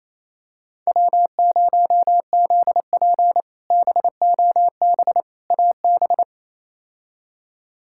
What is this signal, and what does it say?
Morse code, heard as a keyed tone, about 28 wpm, 700 Hz. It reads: W0ZP BOB AB
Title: K1USN SST QSO Practice - 28wpm